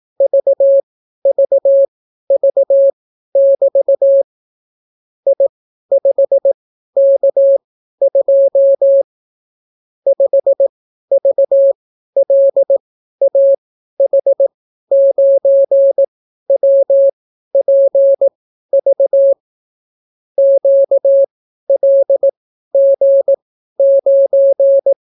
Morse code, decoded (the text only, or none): VVV<BT> I5K2 5VLAH9WPV QLG9